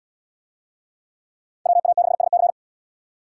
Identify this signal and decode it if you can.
Morse code, heard as a keyed tone, about 70 wpm, 700 Hz. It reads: WA6SC